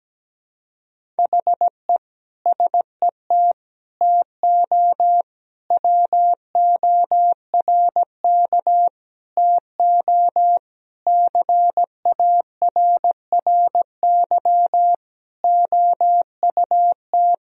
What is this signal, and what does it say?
Morse code, heard as a keyed tone, about 17 wpm, 700 Hz. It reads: HE SET TO WORK TO CARRY OUT
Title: Sentences from Top 1000 Words - Review Part 1 - 17wpm